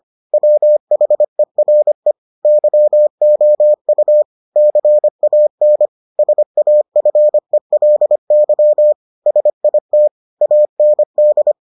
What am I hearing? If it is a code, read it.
Morse code, heard as a keyed tone, about 25 wpm, 600 Hz. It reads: WHERE YOU CAN SAFELY SIT AND